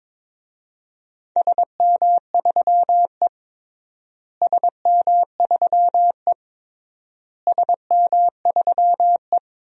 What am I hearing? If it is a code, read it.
Morse code, heard as a keyed tone, about 22 wpm, 700 Hz. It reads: SM3E SM3E SM3E
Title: DXCC-100 Call Signs - 4 Characters - Repeated 3x - 22wpm